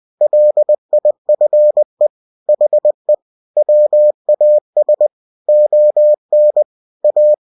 Morse code, heard as a keyed tone, about 20 wpm, 600 Hz. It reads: LIFE HE WAS ON A
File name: cw-books / winnie-the-pooh / winnie-20wpm-page-0096